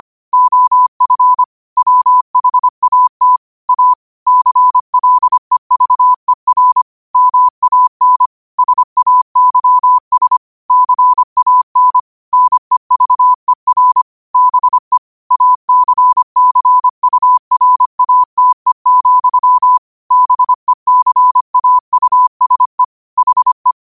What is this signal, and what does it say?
Morse code, heard as a keyed tone, about 25 wpm, 1000 Hz. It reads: OF WHAT A CLEVER MAN SAYS CAN NEVER BE ACCURATE, BECAUSE HE
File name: 25wpm